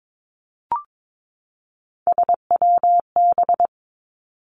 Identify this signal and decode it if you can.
Morse code, heard as a keyed tone, about 22 wpm, 700 Hz. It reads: SWB
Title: Three Challenging Letter-Number Combinations 22wpm